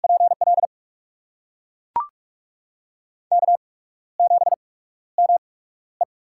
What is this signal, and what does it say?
Morse code, heard as a keyed tone, about 41 wpm, spaced as for 15 wpm, 700 Hz. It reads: PL K7ME
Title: USA Call Signs - 4 Characters - 8x Character Spacing - 45wpm